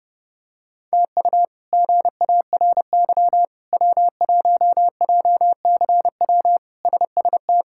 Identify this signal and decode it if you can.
Morse code, heard as a keyed tone, about 30 wpm, 700 Hz. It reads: TU GARY W1JCW SST
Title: K1USN SST QSO Practice - 30wpm